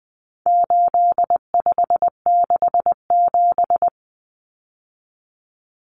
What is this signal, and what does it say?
Morse code, heard as a keyed tone, about 20 wpm, 700 Hz. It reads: 8567
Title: Numbers - 4-Digit 20wpm